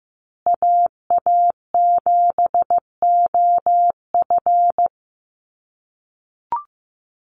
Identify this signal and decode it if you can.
Morse code, heard as a keyed tone, about 15 wpm, 700 Hz. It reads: AA7OF
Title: USA Call Signs Repeated 3x 15wpm